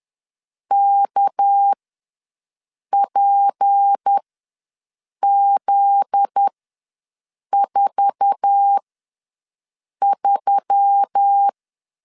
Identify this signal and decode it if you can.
Morse code, heard as a keyed tone, about 10 wpm, spaced as for 6 wpm, 800 Hz. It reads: KPZ43
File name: LEZIONI / 8 YJ45 / 8 YJ4 ws 10wpm 4020